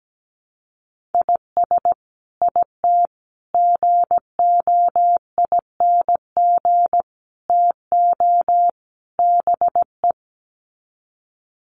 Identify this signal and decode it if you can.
Morse code, heard as a keyed tone, about 17 wpm, 700 Hz. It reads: IS IT GOING TO BE
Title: Sets of 5 Words Encore2 17wpm